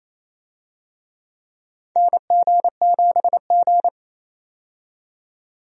Morse code, heard as a keyed tone, about 27 wpm, 700 Hz. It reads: NG7G
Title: USA Call Signs - 4 Characters - 28wpm